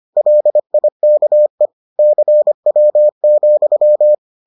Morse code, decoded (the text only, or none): LIKE CW,